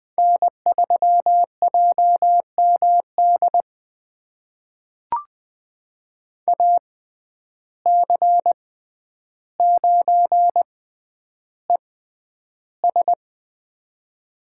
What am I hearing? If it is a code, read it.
Morse code, heard as a keyed tone, about 20 wpm, spaced as for 10 wpm, 700 Hz. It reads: N3JMD AC9ES
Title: USA Call Signs - 5 Characters - 6x Character Spacing - 20wpm